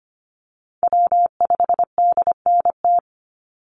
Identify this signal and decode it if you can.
Morse code, heard as a keyed tone, about 25 wpm, 700 Hz. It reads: W5DNT
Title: Call Signs Repeated 2x - 25wpm